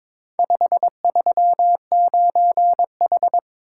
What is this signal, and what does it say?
Morse code, heard as a keyed tone, about 22 wpm, 700 Hz. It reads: 539H
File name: practice / 22wpm / Four Letter-Number Combinations - 22wpm